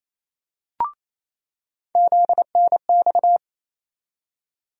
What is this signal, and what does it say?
Morse code, heard as a keyed tone, about 28 wpm, 700 Hz. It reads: ZNX
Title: USA Call Sign Postfixes - 3 Letters 28wpm